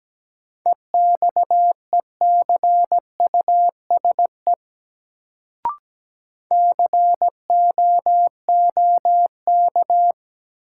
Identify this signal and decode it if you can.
Morse code, heard as a keyed tone, about 17 wpm, 700 Hz. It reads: EXECUSE COOK
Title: Sentences from Top 700 Words - New Word Drill - 17wpm